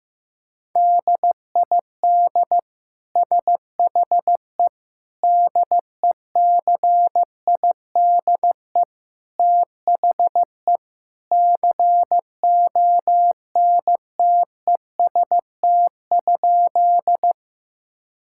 Morse code, heard as a keyed tone, about 15 wpm, 700 Hz. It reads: DID SHE DECIDE THE CONTEST?
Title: Sentences from Top 1600 Words 15wpm